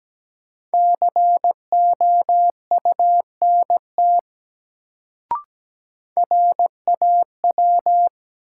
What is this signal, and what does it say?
Morse code, heard as a keyed tone, about 17 wpm, 700 Hz. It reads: COUNT RAW
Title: Sentences from Top 1500 Words - New Word Drill Alt - 17wpm